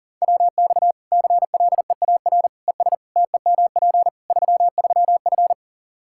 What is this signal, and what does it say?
Morse code, heard as a keyed tone, about 37 wpm, 700 Hz. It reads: WX CLEAR ES TEMP 33F
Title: Practice QSO - Exchange 2 - Info - 40wpm